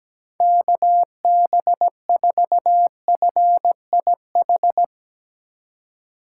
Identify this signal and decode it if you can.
Morse code, heard as a keyed tone, about 17 wpm, 700 Hz. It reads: KB4FIH
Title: Speed Racing - USA 2x3 Call Signs 17wpm